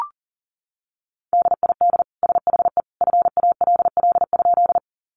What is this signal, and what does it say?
Morse code, heard as a keyed tone, about 40 wpm, 700 Hz. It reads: DID SHE FALL?
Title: Sentences from Top 500 Words 40wpm